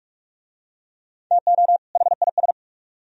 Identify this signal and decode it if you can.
Morse code, heard as a keyed tone, about 45 wpm, 700 Hz. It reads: TO HIS